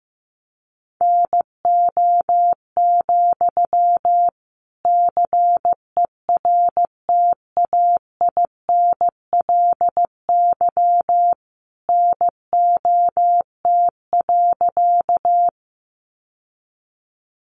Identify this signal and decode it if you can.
Morse code, heard as a keyed tone, about 15 wpm, 700 Hz. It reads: NO, CERTAINLY NOT.
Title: Sentences from Top 900 Words 15wpm